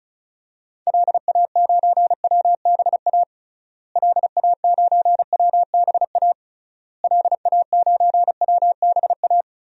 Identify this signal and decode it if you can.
Morse code, heard as a keyed tone, about 36 wpm, 700 Hz. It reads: LA9WBA LA9WBA LA9WBA
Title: DXCC-100 Call Signs - 6 Characters - Repeated 3x - 35wpm